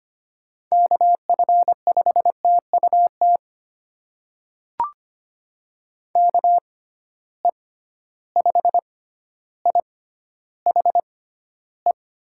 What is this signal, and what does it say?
Morse code, heard as a keyed tone, about 25 wpm, spaced as for 13 wpm, 700 Hz. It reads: KF5TUT KE5IHE
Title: USA Call Signs - 6 Characters - 6x Character Spacing - 25wpm